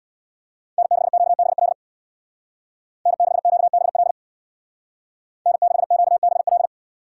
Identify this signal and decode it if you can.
Morse code, heard as a keyed tone, about 60 wpm, 700 Hz. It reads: N6QBL N6QBL N6QBL
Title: Speed Racing - USA 1x3 Call Signs 50wpm